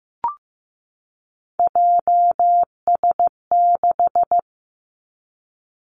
Morse code, heard as a keyed tone, about 15 wpm, 700 Hz. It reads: JS6
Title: Three Challenging Letter-Number Combinations 15wpm